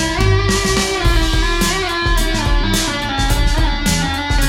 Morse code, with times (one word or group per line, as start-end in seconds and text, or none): none